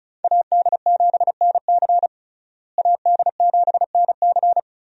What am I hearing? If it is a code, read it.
Morse code, heard as a keyed tone, about 36 wpm, 700 Hz. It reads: AD7NC AD7NC
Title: USA Call Signs - 5 Characters - Repeated 3x - 35wpm